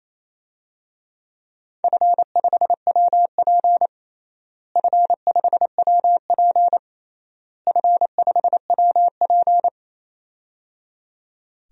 Morse code, heard as a keyed tone, about 28 wpm, 700 Hz. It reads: F5WP F5WP F5WP
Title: DXCC-100 Call Signs - 4 Characters - Repeated 3x - 28wpm